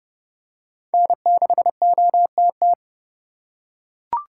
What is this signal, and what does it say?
Morse code, heard as a keyed tone, about 32 wpm, 700 Hz. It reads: N6OTT
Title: USA Call Signs - 5 Characters - 4x Character Spacing - 30wpm